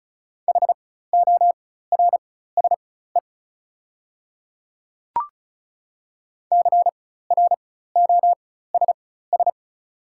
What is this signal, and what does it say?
Morse code, heard as a keyed tone, about 36 wpm, spaced as for 16 wpm, 690 Hz. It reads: HORSE CROSS
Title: 4x Character Spacing - Top 300 Words - 35wpm